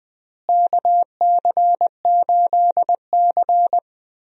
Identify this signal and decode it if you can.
Morse code, heard as a keyed tone, about 20 wpm, 700 Hz. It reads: KC8C
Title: USA Call Signs - 4 Characters - Repeated 3x - 20wpm